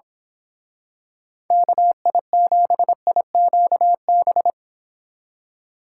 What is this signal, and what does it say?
Morse code, heard as a keyed tone, about 25 wpm, 700 Hz. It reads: KI7IQB
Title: Speed Racing - USA 2x3 Call Signs 22wpm